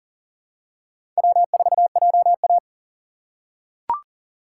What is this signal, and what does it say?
Morse code, heard as a keyed tone, about 40 wpm, 700 Hz. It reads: W4JA
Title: USA Call Signs - 4 Characters - 3x Character Spacing - 40wpm